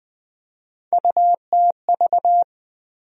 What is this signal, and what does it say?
Morse code, heard as a keyed tone, about 21 wpm, 700 Hz. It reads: UTV
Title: USA Call Sign Postfixes - 3 Letters 20wpm